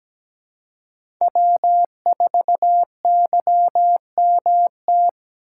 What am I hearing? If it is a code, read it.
Morse code, heard as a keyed tone, about 17 wpm, 700 Hz. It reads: W4YMT
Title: USA Call Signs - 5 Characters - 4x Character Spacing - 17wpm